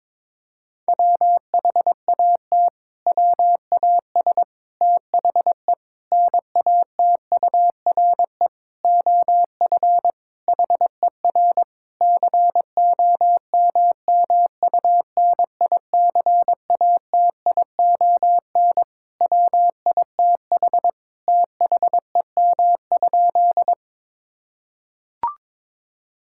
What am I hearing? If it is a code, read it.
Morse code, heard as a keyed tone, about 22 wpm, 700 Hz. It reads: WHAT WAS THE NATURE OF HER COMMUNICATION WITH THEM?
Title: Sentences from Top 1000 Words - Review Part 6 - 22wpm